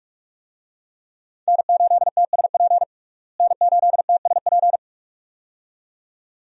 Morse code, heard as a keyed tone, about 46 wpm, 700 Hz. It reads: N8TSP N8TSP
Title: USA Call Signs - 5 Characters - Repeated 2x - 45wpm